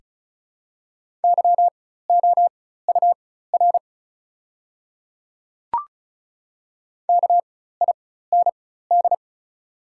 Morse code, heard as a keyed tone, about 36 wpm, spaced as for 16 wpm, 690 Hz. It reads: YOUR KIND
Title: 4x Character Spacing - Top 200 Words - 35wpm